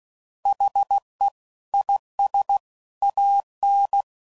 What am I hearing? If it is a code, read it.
Morse code, heard as a keyed tone, about 16 wpm, 800 Hz. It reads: HE IS AN
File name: Sayings / Quotes / Optimist